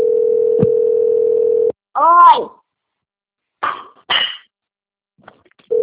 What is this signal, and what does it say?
Ringback tone (North America), 440 + 480 Hz, with speech.
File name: out- - - - - .14385